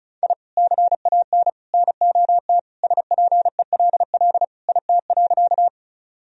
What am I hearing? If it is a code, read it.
Morse code, heard as a keyed tone, about 36 wpm, 700 Hz. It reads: I CAN NOT SPELL IT.
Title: Sentences from Top 1000 Words - Review Part 1 - 35wpm